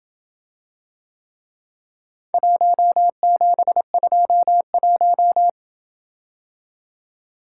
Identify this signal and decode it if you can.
Morse code, heard as a keyed tone, about 26 wpm, 700 Hz. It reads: 1721